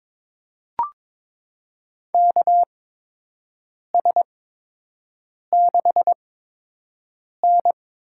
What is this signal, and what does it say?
Morse code, heard as a keyed tone, about 22 wpm, spaced as for 6 wpm, 700 Hz. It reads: KS6N
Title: USA Call Signs - 4 Characters - 8x Character Spacing - 22wpm